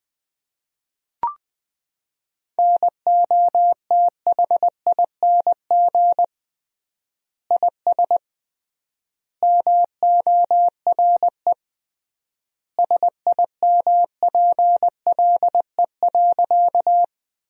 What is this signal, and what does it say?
Morse code, heard as a keyed tone, about 20 wpm, 700 Hz. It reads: NOTHING IS MORE SIMPLE.
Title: 3x Word-Spacing - Sentences from Top 500 Words - 20wpm